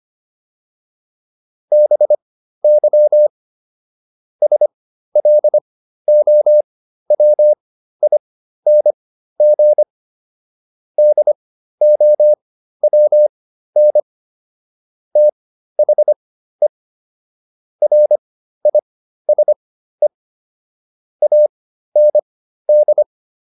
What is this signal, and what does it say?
Morse code, heard as a keyed tone, about 25 wpm, spaced as for 13 wpm, 600 Hz. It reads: BY SLOWING DOWN THE RISE AND